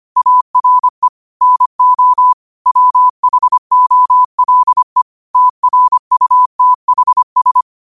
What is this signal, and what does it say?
Morse code, heard as a keyed tone, about 25 wpm, 1000 Hz. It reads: ARE NO WHOLE TRUTHS